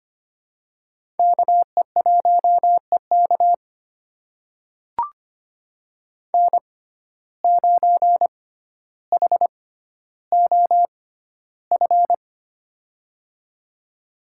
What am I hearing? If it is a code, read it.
Morse code, heard as a keyed tone, about 25 wpm, spaced as for 13 wpm, 700 Hz. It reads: KE1EK N9HOF
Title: USA Call Signs - 5 Characters - 6x Character Spacing - 25wpm